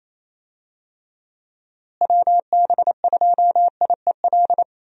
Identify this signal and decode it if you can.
Morse code, heard as a keyed tone, about 28 wpm, 700 Hz. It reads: WB2IEL
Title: USA Call Signs - 6 Characters - Repeated 2x - 28wpm